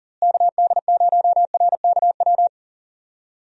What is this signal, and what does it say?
Morse code, heard as a keyed tone, about 40 wpm, 700 Hz. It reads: KD0RKW
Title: USA Call Signs - 6 Characters - Repeated 3x - 40wpm